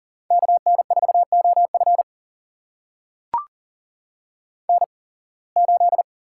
Morse code, heard as a keyed tone, about 39 wpm, spaced as for 26 wpm, 700 Hz. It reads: KN4OF N8
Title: USA Call Signs - 5 Characters - 8x Character Spacing - 40wpm